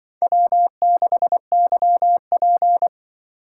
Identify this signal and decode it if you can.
Morse code, heard as a keyed tone, about 25 wpm, 700 Hz. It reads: W6YP
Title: Speed Racing - USA Call Signs 20wpm